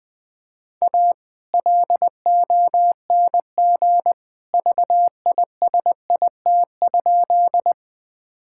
Morse code, heard as a keyed tone, about 20 wpm, 700 Hz. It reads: A LONG VISIT?
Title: Sentences from Top 1000 Words - Review Part 6 - 20wpm